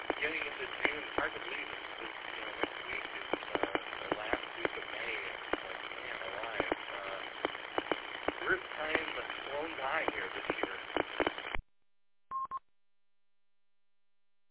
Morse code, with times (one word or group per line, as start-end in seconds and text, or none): none